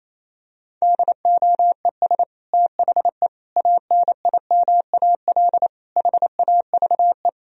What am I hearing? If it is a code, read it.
Morse code, heard as a keyed tone, about 28 wpm, 700 Hz. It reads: DOES THE ANIMAL HAVE
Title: Sentences from Top 1000 Words - Review Part 1 - 28wpm